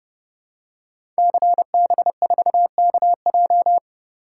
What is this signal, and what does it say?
Morse code, heard as a keyed tone, about 30 wpm, 700 Hz. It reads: CB4KJ
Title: Code Groups - 5 Letter:Numbers - 30wpm